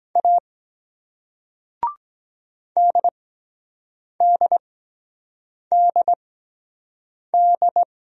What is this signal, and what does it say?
Morse code, heard as a keyed tone, about 23 wpm, spaced as for 7 wpm, 700 Hz. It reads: A DDDD